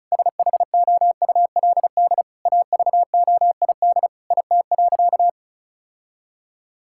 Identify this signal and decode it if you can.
Morse code, heard as a keyed tone, about 36 wpm, 700 Hz. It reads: SHOULD AVOID IT.
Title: Sentences from Top 1000 Words - Review Part 6 - 35wpm